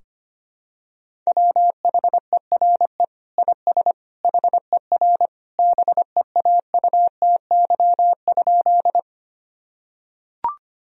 Morse code, heard as a keyed tone, about 25 wpm, 700 Hz. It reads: WHERE IS HER BEAUTY?
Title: Sentences from Top 1000 Words - Review Part 2 - 25wpm